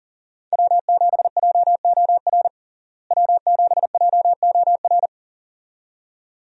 Morse code, heard as a keyed tone, about 40 wpm, 700 Hz. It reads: W7JOR W7JOR
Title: USA Call Signs - 5 Characters - Repeated 2x - 40wpm